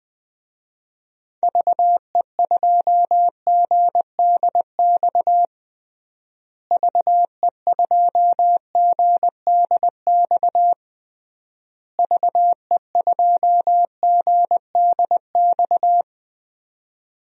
Morse code, heard as a keyed tone, about 20 wpm, 700 Hz. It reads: VE2GDX VE2GDX VE2GDX
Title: DXCC-100 Call Signs - 6 Characters - Repeated 3x - 20wpm